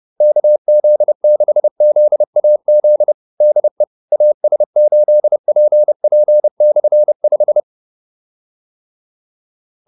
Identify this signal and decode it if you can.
Morse code, heard as a keyed tone, about 30 wpm, 600 Hz. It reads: KZ6ZAZ DE AS8PP/5
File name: QSO302 30